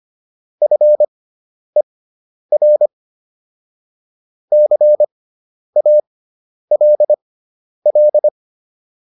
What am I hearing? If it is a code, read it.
Morse code, heard as a keyed tone, about 25 wpm, spaced as for 10 wpm, 600 Hz. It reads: FER CALL
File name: QSO409 10